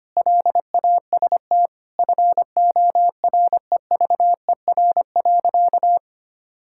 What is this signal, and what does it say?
Morse code, heard as a keyed tone, about 25 wpm, 700 Hz. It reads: LAST FOREVER.
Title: Sentences from Top 1600 Words 25wpm